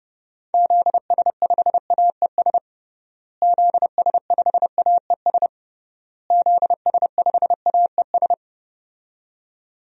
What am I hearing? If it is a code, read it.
Morse code, heard as a keyed tone, about 30 wpm, 700 Hz. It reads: ZS5AES ZS5AES ZS5AES
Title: DXCC-100 Call Signs - 6 Characters - Repeated 3x - 30wpm